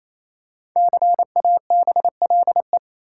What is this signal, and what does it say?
Morse code, heard as a keyed tone, about 28 wpm, 700 Hz. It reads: CABLE
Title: Five Letter Words - 28wpm